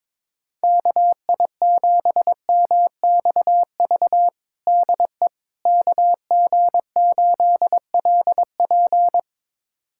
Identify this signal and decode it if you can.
Morse code, heard as a keyed tone, about 22 wpm, 700 Hz. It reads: KI7MXV DE KG8LP
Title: USA Callsign DE Callsign 22wpm